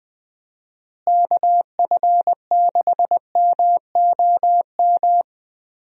Morse code, heard as a keyed tone, about 20 wpm, 700 Hz. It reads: KF6MOM
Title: USA Call Signs - 6 Characters - Repeated 3x - 20wpm